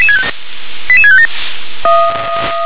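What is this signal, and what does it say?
Five-tone selective call (ZVEI), heard as quick runs of five tones, tones of about 70 ms. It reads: 88657
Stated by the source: ZVEI five-tone selective call received by SDR.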